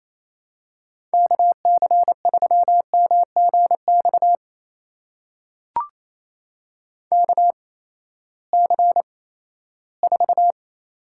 Morse code, heard as a keyed tone, about 28 wpm, spaced as for 16 wpm, 700 Hz. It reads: KC3MGX KC4
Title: USA Call Signs - 6 Characters - 8x Character Spacing - 28wpm